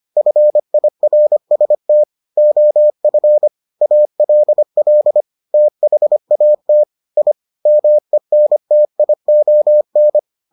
Morse code, heard as a keyed tone, about 25 wpm, 600 Hz. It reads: FIRST OF ALL THAT I MENTION